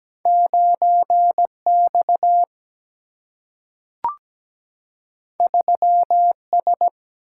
Morse code, heard as a keyed tone, about 17 wpm, 700 Hz. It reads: 9X 3S
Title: Single-Letter Single-Number 17wpm